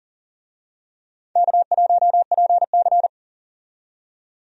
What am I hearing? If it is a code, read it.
Morse code, heard as a keyed tone, about 40 wpm, 700 Hz. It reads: K1PC
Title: USA Call Signs - 4 Characters - 40wpm